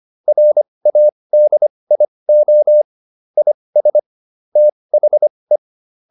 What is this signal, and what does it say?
Morse code, heard as a keyed tone, about 25 wpm, spaced as for 20 wpm, 600 Hz. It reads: RADIO IS THE